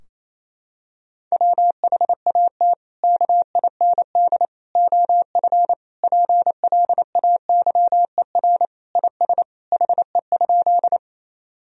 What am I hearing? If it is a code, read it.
Morse code, heard as a keyed tone, about 28 wpm, 700 Hz. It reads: WHAT KIND OF PLAYER IS HE?
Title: Sentences from Top 1000 Words - Review Part 4 - 28wpm